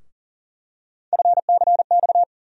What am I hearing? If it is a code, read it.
Morse code, heard as a keyed tone, about 40 wpm, 700 Hz. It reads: FCX
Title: USA Call Sign Postfixes - 3 Letters 40wpm